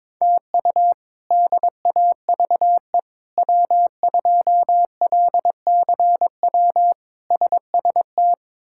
Morse code, heard as a keyed tone, about 22 wpm, 700 Hz. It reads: TU DAVE W2LCW SST
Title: K1USN SST QSO Practice - 22wpm